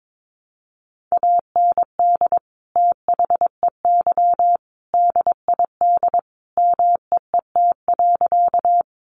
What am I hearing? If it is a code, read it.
Morse code, heard as a keyed tone, about 22 wpm, 700 Hz. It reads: AND THEY DID MEET.
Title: Sentences from Top 1100 Words 22wpm